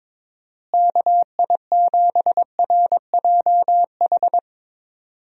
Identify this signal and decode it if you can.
Morse code, heard as a keyed tone, about 22 wpm, 700 Hz. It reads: KI7RJH
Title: USA Call Signs - 6 Characters - Repeated 3x - 22wpm